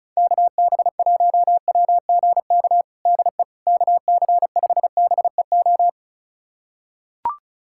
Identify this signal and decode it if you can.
Morse code, heard as a keyed tone, about 34 wpm, 690 Hz. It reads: KB1WGK DE KC5BEO
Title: USA Callsign DE Callsign - 6 Characters - 35wpm